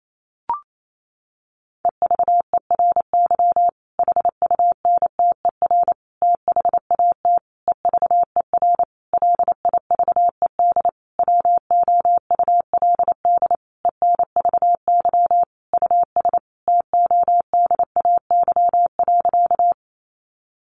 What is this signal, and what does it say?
Morse code, heard as a keyed tone, about 28 wpm, 700 Hz. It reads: EVERY HUNTER THAT EVER LIVED WOULD ENVY US TODAY.
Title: A Sound Of Thunder 28wpm